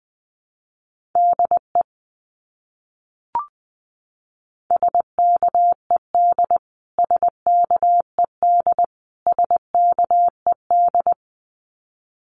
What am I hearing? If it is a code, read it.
Morse code, heard as a keyed tone, about 20 wpm, 700 Hz. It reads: DE SKED SKED SKED
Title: Abbreviations Repeated 3x - 20wpm